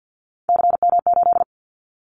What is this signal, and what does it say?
Morse code, heard as a keyed tone, about 50 wpm, 690 Hz. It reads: CM8
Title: Callsign Prefixes for Top 100 DXCC - 50wpm